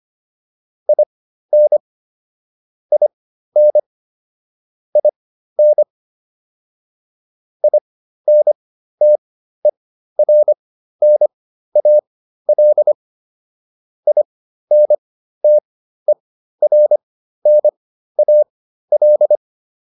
Morse code, heard as a keyed tone, about 25 wpm, spaced as for 13 wpm, 600 Hz. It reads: IN IN IN INTERNAL INTERNAL